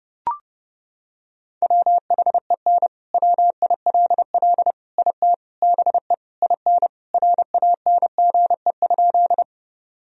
Morse code, heard as a keyed tone, about 30 wpm, 700 Hz. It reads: WHEN WILL IT BE IN RANGE?
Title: Sentences from Top 1000 Words - Review Part 4 - 30wpm